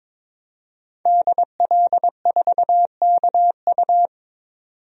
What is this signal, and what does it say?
Morse code, heard as a keyed tone, about 22 wpm, 700 Hz. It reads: DL4KU
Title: DXCC-100 Call Signs - 5 Characters - 22wpm